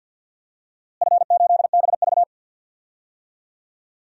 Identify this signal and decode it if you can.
Morse code, heard as a keyed tone, about 50 wpm, 700 Hz. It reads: F8BV
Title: DXCC-100 Call Signs - 4 Characters - Repeated 2x - 50wpm